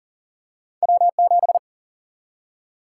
Morse code, heard as a keyed tone, about 40 wpm, 700 Hz. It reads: W 7
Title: Single-Letter Single-Number 40wpm